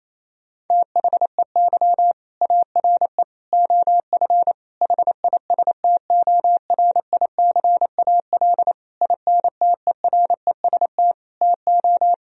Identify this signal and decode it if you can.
Morse code, heard as a keyed tone, about 28 wpm, 700 Hz. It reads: THEY ARE OF HISTORICAL INTEREST TO
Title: Sentences from Top 1000 Words - Review Part 4 - 28wpm